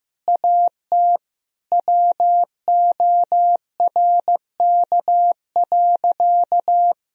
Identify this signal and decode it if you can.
Morse code, heard as a keyed tone, about 15 wpm, 700 Hz. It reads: AT WORK.
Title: Sentences from Top 1000 Words - Review Part 3 - 15wpm